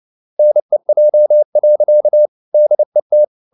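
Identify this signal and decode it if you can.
Morse code, heard as a keyed tone, about 29 wpm, 600 Hz. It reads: NEJ. DET
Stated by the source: Fast (QRQ) Morse at 29 wpm, from CW med Gnister.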